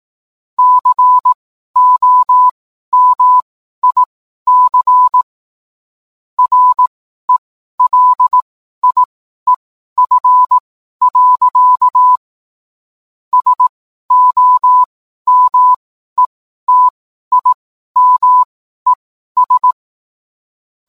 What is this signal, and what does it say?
Morse code, heard as a keyed tone, about 18 wpm, spaced as for 12 wpm, 1000 Hz. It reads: COMIC RELIEF. SOMETIMES
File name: cwblog / cw mp3 directory / mem morse 12